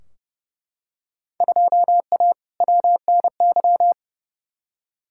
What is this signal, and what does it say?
Morse code, heard as a keyed tone, about 30 wpm, 700 Hz. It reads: 2A WNY
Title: Field Day Exchanges - 30wpm